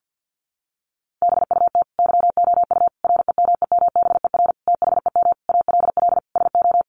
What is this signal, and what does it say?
Morse code, heard as a keyed tone, about 50 wpm, 700 Hz. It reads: BUT YOU REMEMBER THEM ALL SO
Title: Sentences from Top 1300 Words 50wpm